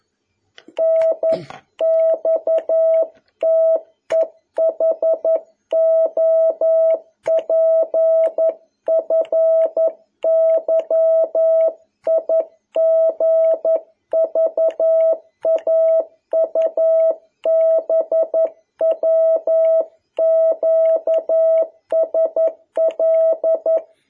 Morse code, heard as a keyed tone, about 11 wpm, 650 Hz. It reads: NXTEHOPFYIGVAUBWQSL
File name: jh8cbh / 2021cw